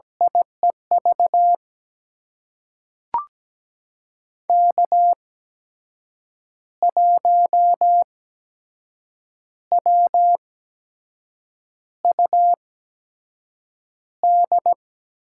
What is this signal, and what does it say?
Morse code, heard as a keyed tone, about 17 wpm, spaced as for 6 wpm, 700 Hz. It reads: IEV K1WUD
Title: USA Call Signs - 8x Character Spacing - 17wpm